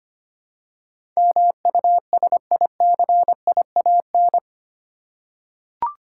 Morse code, heard as a keyed tone, about 25 wpm, 700 Hz. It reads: MUSICIAN
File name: 8 Letter Words - First 3 Letters Given - 25wpm